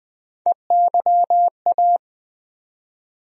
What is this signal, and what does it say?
Morse code, heard as a keyed tone, about 20 wpm, 700 Hz. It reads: EYA
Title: USA Call Sign Postfixes - 3 Letters 20wpm